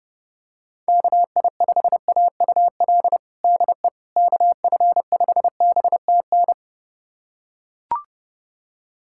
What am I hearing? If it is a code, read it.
Morse code, heard as a keyed tone, about 31 wpm, 700 Hz. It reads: KI5AUL DE KF5BTN
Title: USA Callsign DE Callsign - 6 Characters - 30wpm